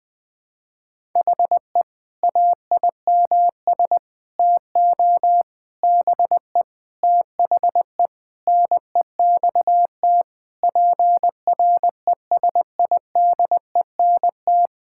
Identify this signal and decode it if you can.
Morse code, heard as a keyed tone, about 19 wpm, 700 Hz. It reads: HE AIMS TO BE THE NEXT PRESIDENT
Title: Sentences with 8-Word Phrases - 20wpm